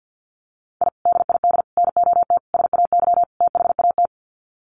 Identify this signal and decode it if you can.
Morse code, heard as a keyed tone, about 50 wpm, 700 Hz. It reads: I DID NOT SAY THAT
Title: Sets of 5 Words Encore2 50wpm